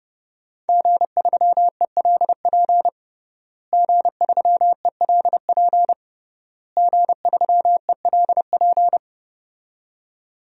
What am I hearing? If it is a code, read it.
Morse code, heard as a keyed tone, about 30 wpm, 700 Hz. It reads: G3ELP G3ELP G3ELP
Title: DXCC-100 Call Signs - 5 Characters - Repeated 3x - 30wpm